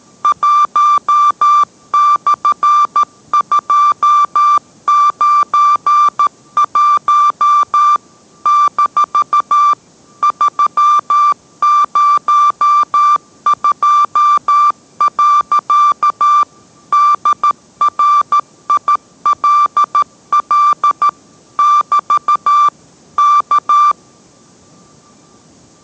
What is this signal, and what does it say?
Morse code, heard as a keyed tone, about 13 wpm, 1180 Hz. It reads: 1/291 - 302. DRILL <BT> K